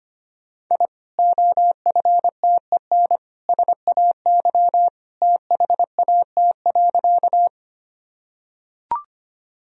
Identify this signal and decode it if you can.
Morse code, heard as a keyed tone, about 25 wpm, 700 Hz. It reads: I OFTEN SAY THAT.
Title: Sentences from Top 1000 Words - Review Part 1 - 25wpm